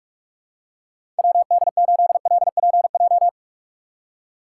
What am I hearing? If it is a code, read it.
Morse code, heard as a keyed tone, about 46 wpm, 700 Hz. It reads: WD8LPJ
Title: USA Call Signs - 6 Characters - 4x Character Spacing - 45wpm